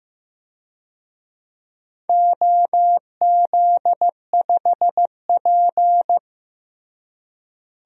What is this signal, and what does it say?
Morse code, heard as a keyed tone, about 15 wpm, 700 Hz. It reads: OZ5P
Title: DXCC-100 Call Signs - 4 Characters - Repeated 3x - 15wpm